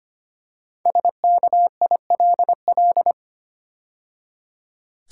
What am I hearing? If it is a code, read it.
Morse code, heard as a keyed tone, about 25 wpm, 700 Hz. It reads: SKILL
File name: 5 Letter Words - First 3 Letters Given - 25wpm